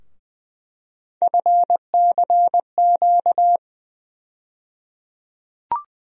Morse code, heard as a keyed tone, about 20 wpm, 700 Hz. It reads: FCQ
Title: USA Call Sign Postfixes - 3 Letters 20wpm